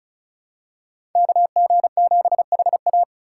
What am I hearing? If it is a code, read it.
Morse code, heard as a keyed tone, about 34 wpm, 700 Hz. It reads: KG7HA
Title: USA Call Signs - 4x Character Spacing - 35wpm